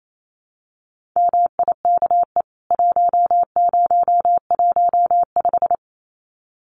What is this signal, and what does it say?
Morse code, heard as a keyed tone, about 28 wpm, 700 Hz. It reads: MIKE 1015
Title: CWT Exchange 28wpm